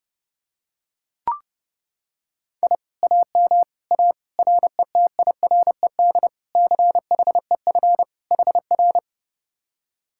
Morse code, heard as a keyed tone, about 31 wpm, 700 Hz. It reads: I AM A RETIRED CHEF HR